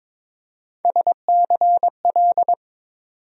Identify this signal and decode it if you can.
Morse code, heard as a keyed tone, about 22 wpm, 700 Hz. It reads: SCL
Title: USA Call Sign Postfixes - 3 Letters 22wpm